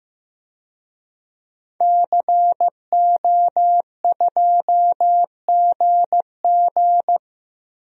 Morse code, heard as a keyed tone, about 15 wpm, 700 Hz. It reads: CO2GG
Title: DXCC-100 Call Signs - 5 Characters - Repeated 3x - 15wpm